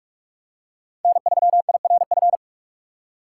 Morse code, heard as a keyed tone, about 46 wpm, 700 Hz. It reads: N 3 I R F
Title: USA Call Signs - 5 Characters - 6x Character Spacing - 45wpm